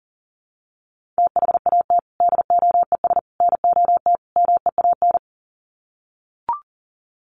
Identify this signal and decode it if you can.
Morse code, heard as a keyed tone, about 39 wpm, 700 Hz. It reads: THAT DOES NOT MEAN
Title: Sets of 4 Words 40wpm